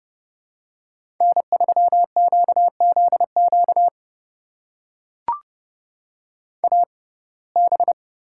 Morse code, heard as a keyed tone, about 30 wpm, spaced as for 22 wpm, 700 Hz. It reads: N3QZQ AB
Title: USA Call Signs - 5 Characters - 6x Character Spacing - 30wpm